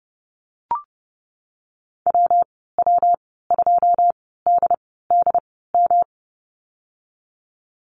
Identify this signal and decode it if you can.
Morse code, heard as a keyed tone, about 32 wpm, spaced as for 17 wpm, 700 Hz. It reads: WW2DDM
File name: Lesson 19g - Letter w - Call Signs 30wpm